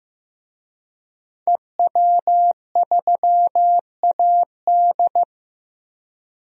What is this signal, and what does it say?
Morse code, heard as a keyed tone, about 15 wpm, 700 Hz. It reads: EW3AD
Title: DXCC-100 Call Signs - 5 Characters - 15wpm